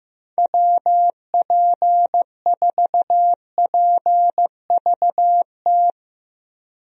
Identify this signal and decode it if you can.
Morse code, heard as a keyed tone, about 15 wpm, 700 Hz. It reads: WP4PVT
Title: USA Call Signs - 6 Characters - Repeated 2x - 15wpm